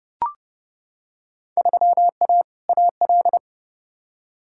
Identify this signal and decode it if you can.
Morse code, heard as a keyed tone, about 30 wpm, 700 Hz. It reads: 3A AL
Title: Field Day Exchanges - 30wpm